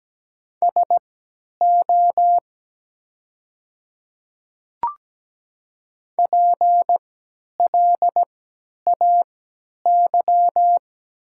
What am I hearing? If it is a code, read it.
Morse code, heard as a keyed tone, about 17 wpm, spaced as for 10 wpm, 700 Hz. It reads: SO PLAY
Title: 3x Character Spacing - Top 200 Words - 17wpm